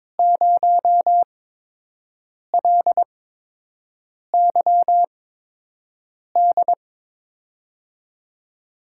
Morse code, heard as a keyed tone, about 23 wpm, 700 Hz. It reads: 0 L Y D
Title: USA Call Signs - 6 Characters - 8x Character Spacing - 22wpm